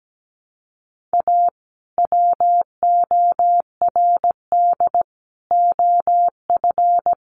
Morse code, heard as a keyed tone, about 17 wpm, 700 Hz. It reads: A WORD OF
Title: Sentences from Top 1100 Words 17wpm